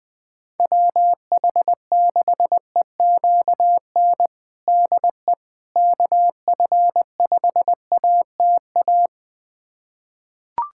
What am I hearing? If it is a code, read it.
Morse code, heard as a keyed tone, about 21 wpm, 700 Hz. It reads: WH6EQN DE KF5ATA
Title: USA Callsign DE Callsign - 6 Characters - 20wpm